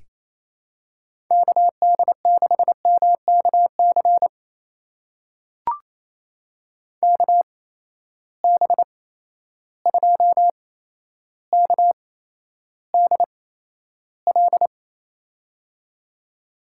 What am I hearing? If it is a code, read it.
Morse code, heard as a keyed tone, about 27 wpm, spaced as for 12 wpm, 700 Hz. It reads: KD6MKC KB2KDL
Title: USA Call Signs - 6 Characters - 8x Character Spacing - 28wpm